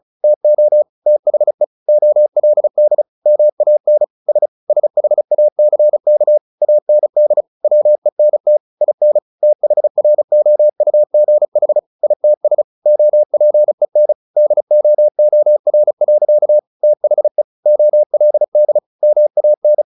Morse code, heard as a keyed tone, about 34 wpm, 600 Hz. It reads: TO THE OLD MAN S SHACK AND WENT IN THROUGH ITS OPEN DOOR. THE OLD MAN
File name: the old man and the sea / the old man and the sea 1 35wpm